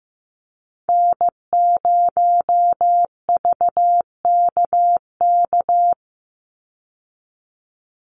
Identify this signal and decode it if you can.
Morse code, heard as a keyed tone, about 15 wpm, 700 Hz. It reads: N0VKK
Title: USA Call Signs 15wpm